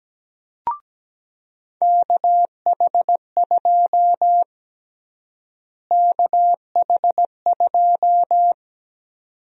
Morse code, heard as a keyed tone, about 17 wpm, 700 Hz. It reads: KH2 KH2
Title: USA Call Sign Prefixes - Repeated 2x - 17wpm